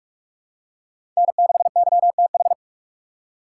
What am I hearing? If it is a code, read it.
Morse code, heard as a keyed tone, about 46 wpm, 700 Hz. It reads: N6YTH
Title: USA Call Signs - 4x Character Spacing - 45wpm